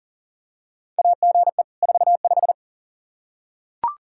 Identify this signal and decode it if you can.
Morse code, heard as a keyed tone, about 40 wpm, 700 Hz. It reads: AGE 45
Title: Practice QSO - Exchange 3 - More Info - 40wpm